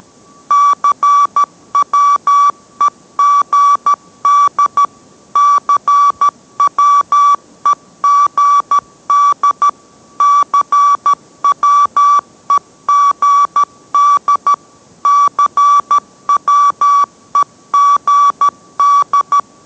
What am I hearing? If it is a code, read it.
Morse code, heard as a keyed tone, about 13 wpm, 1150 Hz. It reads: CWEGD CWEGD CWEGD CWEGD